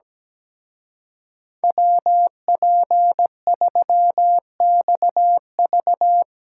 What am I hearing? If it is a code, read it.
Morse code, heard as a keyed tone, about 18 wpm, 700 Hz. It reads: WP3XV
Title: Speed Racing - USA 2x2 Call Signs 17wpm